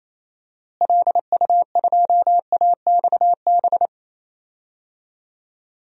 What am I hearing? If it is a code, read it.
Morse code, heard as a keyed tone, about 28 wpm, 700 Hz. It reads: LU2AXB
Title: DXCC-100 Call Signs - 6 Characters - Repeated 3x - 28wpm